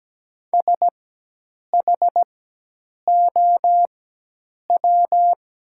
Morse code, heard as a keyed tone, about 17 wpm, spaced as for 8 wpm, 700 Hz. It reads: SHOW
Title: 4x Character Spacing - Top 200 Words - 17wpm